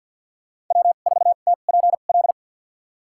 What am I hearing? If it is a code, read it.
Morse code, heard as a keyed tone, about 52 wpm, spaced as for 37 wpm, 690 Hz. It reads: W4TPL
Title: USA Call Signs - 5 Characters - 2x Character Spacing - 50wpm